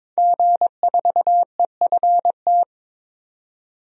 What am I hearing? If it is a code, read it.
Morse code, heard as a keyed tone, about 23 wpm, 700 Hz. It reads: G4EFT
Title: DXCC-100 Call Signs - 5 Characters - 22wpm